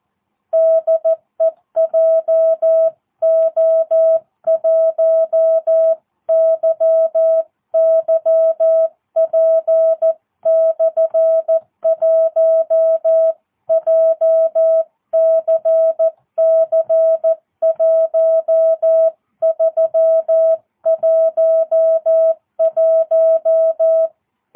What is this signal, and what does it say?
Morse code, heard as a keyed tone, about 14 wpm, 650 Hz. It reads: DE JO1YYP/1 JCC1311
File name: jh8cbh / 2022cw